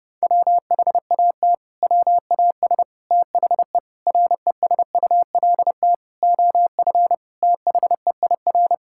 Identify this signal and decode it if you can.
Morse code, heard as a keyed tone, about 32 wpm, 700 Hz. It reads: WHAT WAS THE RESULT OF THEIR
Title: Sentences from Top 1000 Words - Review Part 5 - 30wpm